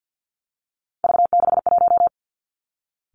Morse code, heard as a keyed tone, about 50 wpm, 700 Hz. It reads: V61
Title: Three Challenging Letter-Number Combinations 50wpm